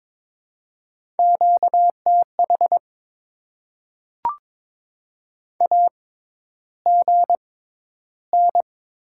Morse code, heard as a keyed tone, about 22 wpm, spaced as for 11 wpm, 700 Hz. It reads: QTH AGN
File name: Top 20 QSO Elements - 6x Character Spacing - 22wpm